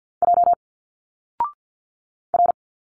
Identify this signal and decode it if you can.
Morse code, heard as a keyed tone, about 50 wpm, 700 Hz. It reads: J R
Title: Single Letters 50wpm